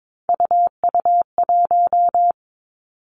Morse code, heard as a keyed tone, about 22 wpm, 700 Hz. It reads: UU1
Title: Three Challenging Letter-Number Combinations 22wpm